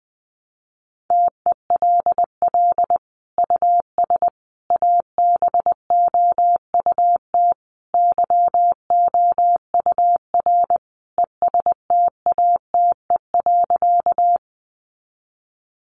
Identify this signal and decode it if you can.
Morse code, heard as a keyed tone, about 20 wpm, 700 Hz. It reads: TELL US ABOUT YOUR ESTATE.
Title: Sentences from Top 1200 Words 20wpm